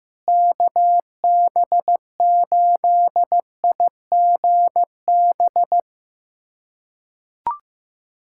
Morse code, heard as a keyed tone, about 15 wpm, 700 Hz. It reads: KB8IGB
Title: USA Callsign DE Callsign - 6 Characters - 15wpm